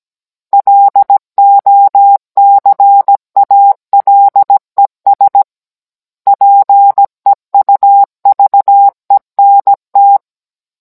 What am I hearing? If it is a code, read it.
Morse code, heard as a keyed tone, about 17 wpm, 800 Hz. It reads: LOCALES PEUVENT
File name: Cours-NR18-16mm